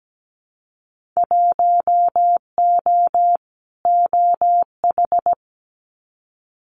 Morse code, heard as a keyed tone, about 17 wpm, 700 Hz. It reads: 1O OH
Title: Winter Field Day Exchanges 17wpm